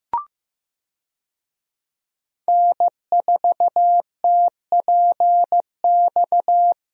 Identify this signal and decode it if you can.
Morse code, heard as a keyed tone, about 15 wpm, 700 Hz. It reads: N4TPX
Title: USA Call Signs - 5 Characters - Repeated 3x - 15wpm